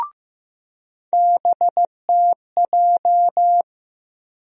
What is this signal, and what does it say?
Morse code, heard as a keyed tone, about 15 wpm, 700 Hz. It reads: BTJ
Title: USA Call Sign Postfixes - 3 Letters 15wpm